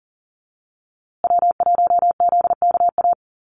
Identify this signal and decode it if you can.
Morse code, heard as a keyed tone, about 40 wpm, 700 Hz. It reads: W1ZKA
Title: USA Call Signs Repeated 3x 40wpm